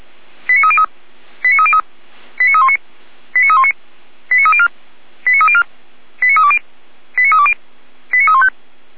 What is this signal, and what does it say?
Five-tone selective call (ZVEI), heard as quick runs of five tones, tones of about 70 ms. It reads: 89303 89303 89319 89319 89304 89304 89320 89320 89315
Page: /455/SDR/ZVEI/89125/